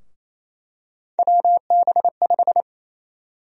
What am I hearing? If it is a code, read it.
Morse code, heard as a keyed tone, about 28 wpm, 700 Hz. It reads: WB5
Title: USA Call Sign Prefixes - 28wpm